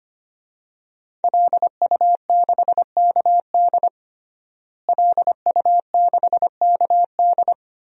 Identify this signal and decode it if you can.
Morse code, heard as a keyed tone, about 25 wpm, 700 Hz. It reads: LU6KD LU6KD
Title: DXCC-100 Call Signs - 5 Characters - Repeated 2x - 25wpm